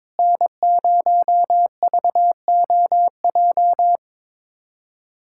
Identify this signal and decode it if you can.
Morse code, heard as a keyed tone, about 22 wpm, 700 Hz. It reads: N0VOJ
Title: USA Callsign DE Callsign - 5 Characters - 22wpm